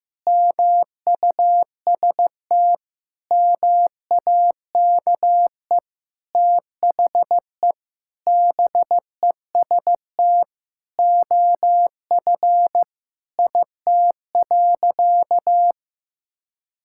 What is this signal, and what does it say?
Morse code, heard as a keyed tone, about 15 wpm, 700 Hz. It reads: MUST MAKE THE BEST OF IT.
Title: Sentences from Top 1000 Words - Review Part 5 - 15wpm